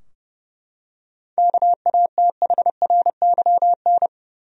Morse code, heard as a keyed tone, about 31 wpm, 700 Hz. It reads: KATHRYN
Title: USA Common Names - 30wpm